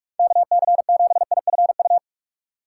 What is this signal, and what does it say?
Morse code, heard as a keyed tone, about 46 wpm, 700 Hz. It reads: KC7IFU
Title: USA Call Signs - 6 Characters - 4x Character Spacing - 45wpm